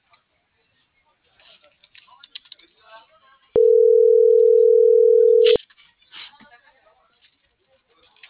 Ringback tone (North America), 440 + 480 Hz.